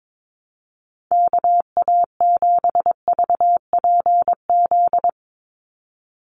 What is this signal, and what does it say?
Morse code, heard as a keyed tone, about 22 wpm, 700 Hz. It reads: KA7VPZ
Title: USA Call Signs 22wpm